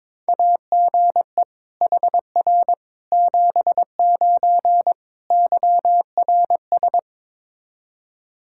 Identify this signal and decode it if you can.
Morse code, heard as a keyed tone, about 21 wpm, 700 Hz. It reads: AGE HR 79 YRS
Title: K1USN SST QSO Practice 22wpm